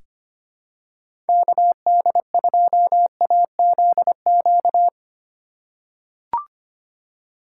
Morse code, heard as a keyed tone, about 25 wpm, 700 Hz. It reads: KD2AZQ
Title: USA Call Signs - 6 Characters - 2x Character Spacing - 25wpm